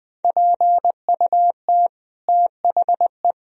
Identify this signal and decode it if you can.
Morse code, heard as a keyed tone, about 20 wpm, 700 Hz. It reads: PUT THE
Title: Sentences from Top 1000 Words - Review Part 4 - 20wpm